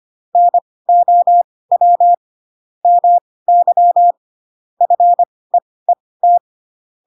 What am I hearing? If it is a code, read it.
Morse code, heard as a keyed tone, about 25 wpm, spaced as for 18 wpm, 700 Hz. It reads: NOW MY FEET
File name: Pinocchio / Pinocchio ch7 12 25 ik1hge